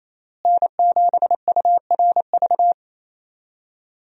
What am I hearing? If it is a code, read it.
Morse code, heard as a keyed tone, about 28 wpm, 700 Hz. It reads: N7URV
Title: USA Call Signs - 5 Characters - Repeated 3x - 28wpm